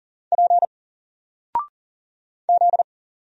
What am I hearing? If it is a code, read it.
Morse code, heard as a keyed tone, about 40 wpm, 700 Hz. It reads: P Z
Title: Single Letter-Number 40wpm